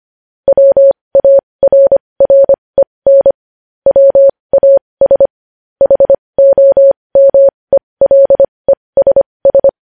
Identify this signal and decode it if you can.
Morse code, heard as a keyed tone, about 25 wpm, spaced as for 20 wpm, 550 Hz. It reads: WARREN WAS HOMELESS
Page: news-headlines-in-morse-code-at-20-wpm